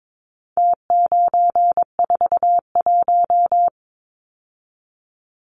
Morse code, heard as a keyed tone, about 22 wpm, 700 Hz. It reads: T941
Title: Numbers - Contest Style - 4-Digit 22wpm